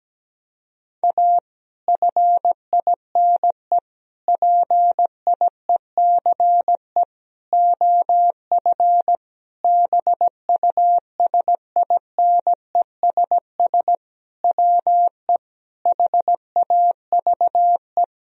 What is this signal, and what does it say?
Morse code, heard as a keyed tone, about 17 wpm, 700 Hz. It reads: A FINE PIECE OF BUSINESS WE HAVE
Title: Sentences from Top 1000 Words - Review Part 2 - 17wpm